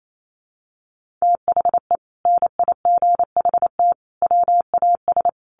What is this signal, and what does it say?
Morse code, heard as a keyed tone, about 28 wpm, 700 Hz. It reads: THE NIGHT WAS
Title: Sentences from Top 1500 Words 28wpm